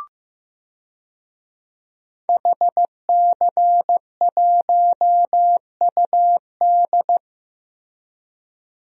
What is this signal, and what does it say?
Morse code, heard as a keyed tone, about 15 wpm, 700 Hz. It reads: HC1UD
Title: DXCC-100 Call Signs - 5 Characters - Repeated 3x - 15wpm